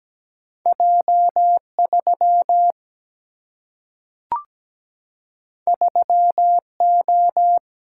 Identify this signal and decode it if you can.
Morse code, heard as a keyed tone, about 17 wpm, 700 Hz. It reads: J 3 3 O
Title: Single-Letter Single-Number 17wpm